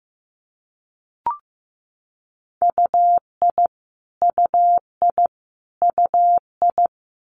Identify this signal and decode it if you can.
Morse code, heard as a keyed tone, about 15 wpm, 700 Hz. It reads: UI UI UI
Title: Multi-letter Phonemes Repeated 3x 15wpm